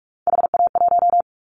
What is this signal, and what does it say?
Morse code, heard as a keyed tone, about 44 wpm, 700 Hz. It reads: HA1
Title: Callsign Prefixes for Top 25 DXCC - 45wpm